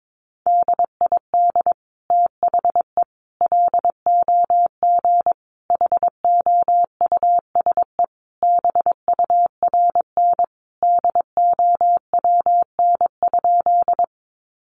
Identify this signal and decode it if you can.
Morse code, heard as a keyed tone, about 22 wpm, 700 Hz. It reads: DID THE LOG HOUSE BURN DOWN?
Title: Sentences from Top 1400 Words 22wpm